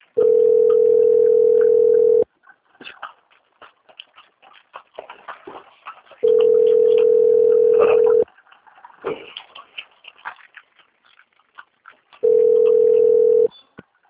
Ringback tone (North America), 440 + 480 Hz.